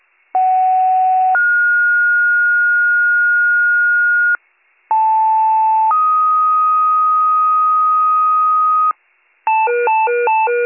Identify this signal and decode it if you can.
Two-tone page, heard as a one-second tone, then a three-second tone, about 750 then 1430 Hz, about 880 then 1250 Hz, with noise. It is heard from 0.3 to 8.9 s.